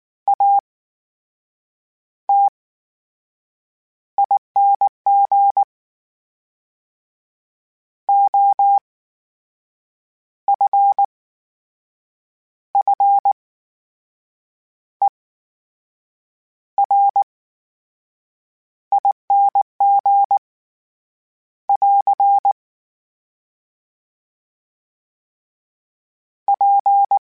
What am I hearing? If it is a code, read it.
Morse code, heard as a keyed tone, about 19 wpm, spaced as for 6 wpm, 800 Hz. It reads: ATING OFFERING <AR> P
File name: Special INGs / Special INGs-mp3s / VerySlowToFaster Ing @18